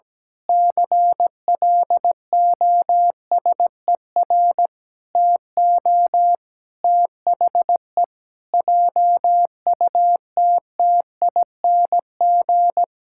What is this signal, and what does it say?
Morse code, heard as a keyed tone, about 17 wpm, 700 Hz. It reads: CLOSER TO THE JUTTING
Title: Pangram Challenge v2 - 17wpm